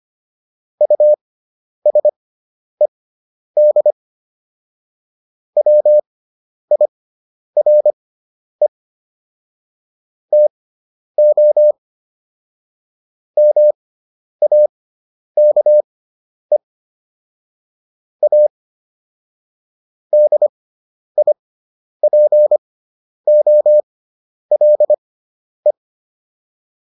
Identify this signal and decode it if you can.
Morse code, heard as a keyed tone, about 25 wpm, spaced as for 10 wpm, 600 Hz. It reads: USED WIRE TO MAKE A DIPOLE